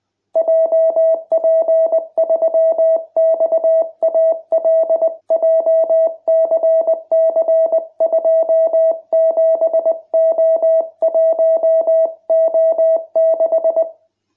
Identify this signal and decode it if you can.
Morse code, heard as a keyed tone, about 20 wpm, 660 Hz. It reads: JP3XAL JCC27O1O6